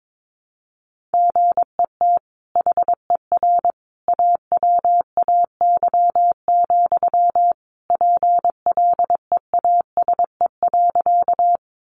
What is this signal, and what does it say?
Morse code, heard as a keyed tone, about 22 wpm, 700 Hz. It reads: GET HER AWAY, PLEASE.
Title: Sentences from Top 1500 Words 22wpm